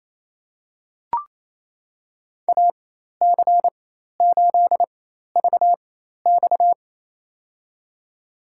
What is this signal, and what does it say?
Morse code, heard as a keyed tone, about 28 wpm, spaced as for 13 wpm, 700 Hz. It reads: AC8VX
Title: USA Call Signs - 5 Characters - 4x Character Spacing - 28wpm